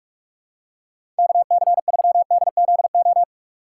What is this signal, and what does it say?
Morse code, heard as a keyed tone, about 45 wpm, 700 Hz. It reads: KC3DZO
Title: USA Call Signs - 6 Characters - 4x Character Spacing - 45wpm